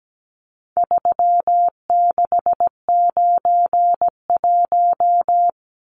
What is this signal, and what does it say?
Morse code, heard as a keyed tone, about 17 wpm, 700 Hz. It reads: 3691
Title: Numbers - 4-Digit 17wpm